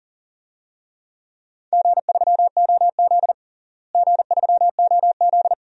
Morse code, heard as a keyed tone, about 40 wpm, 700 Hz. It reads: G3OZ G3OZ
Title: DXCC-100 Call Signs - 4 Characters - Repeated 2x - 40wpm